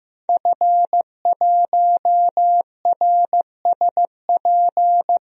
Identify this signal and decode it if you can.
Morse code, heard as a keyed tone, about 15 wpm, 700 Hz. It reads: F1RSP
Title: DXCC-100 Call Signs - 5 Characters - 15wpm